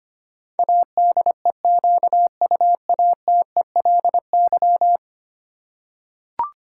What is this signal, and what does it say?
Morse code, heard as a keyed tone, about 25 wpm, 700 Hz. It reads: ADEQUATELY
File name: Ten Letter Words - 25wpm